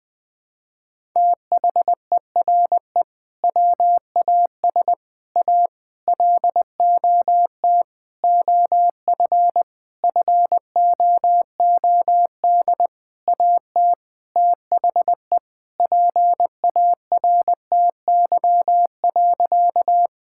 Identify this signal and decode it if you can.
Morse code, heard as a keyed tone, about 21 wpm, 700 Hz. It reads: THERE WAS A LOT OF FOOD AT THE PARTY.
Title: Sentences with 5-Word Phrases - Spoken Beginning - 20wpm